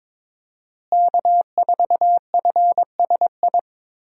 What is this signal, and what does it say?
Morse code, heard as a keyed tone, about 22 wpm, 700 Hz. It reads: K4FSI
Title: USA Call Signs - 5 Characters - Repeated 2x - 22wpm